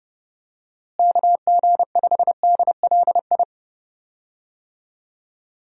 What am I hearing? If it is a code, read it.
Morse code, heard as a keyed tone, about 29 wpm, 700 Hz. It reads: KG5DLI
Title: USA Call Signs - 6 Characters - Repeated 3x - 30wpm